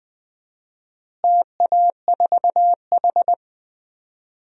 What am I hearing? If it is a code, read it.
Morse code, heard as a keyed tone, about 20 wpm, 700 Hz. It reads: TA4H
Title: DXCC-100 Call Signs - 4 Characters - Repeated 3x - 20wpm